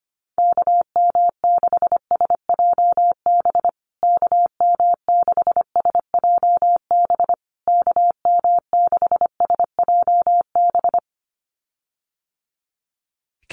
Morse code, heard as a keyed tone, about 25 wpm, 700 Hz. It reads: KM6SJB KM6SJB KM6SJB
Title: USA Call Signs Repeated 3x 25wpm